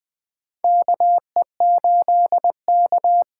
Morse code, heard as a keyed tone, about 20 wpm, 700 Hz. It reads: KE8K
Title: USA Call Signs - 6 Characters - 8x Character Spacing - 20wpm